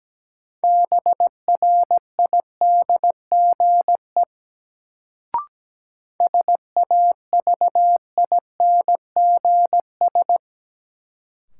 Morse code, heard as a keyed tone, about 17 wpm, 700 Hz. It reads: BRIDGE SAVINGS
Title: Sentences from Top 1200 Words - New Word Drill Alt - 17wpm